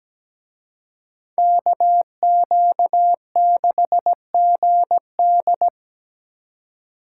Morse code, heard as a keyed tone, about 20 wpm, 700 Hz. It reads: KQ6GD
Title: Speed Racing - USA Call Signs 17wpm